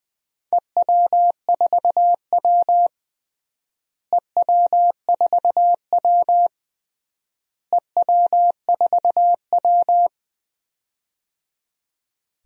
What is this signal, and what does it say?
Morse code, heard as a keyed tone, about 20 wpm, 700 Hz. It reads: EW4W EW4W EW4W
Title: DXCC-100 Call Signs - 4 Characters - Repeated 3x - 20wpm